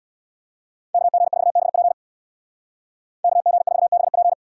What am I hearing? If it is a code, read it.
Morse code, heard as a keyed tone, about 64 wpm, 700 Hz. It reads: KG4BP KG4BP